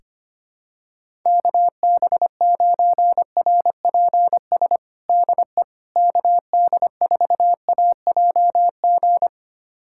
Morse code, heard as a keyed tone, about 25 wpm, 700 Hz. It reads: KB9RPS DE KD4AJG
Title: USA Callsign DE Callsign - 6 Characters - 25wpm